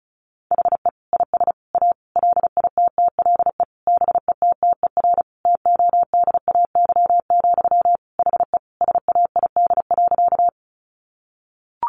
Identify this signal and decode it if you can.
Morse code, heard as a keyed tone, about 34 wpm, 700 Hz. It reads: HE IS A LITTLE BETTER TODAY, HE SAID.
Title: Sentences from Top 600 Words 35wpm